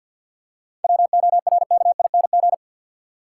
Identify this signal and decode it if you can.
Morse code, heard as a keyed tone, about 47 wpm, 700 Hz. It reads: WORKING
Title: Seven Letter Words - 50wpm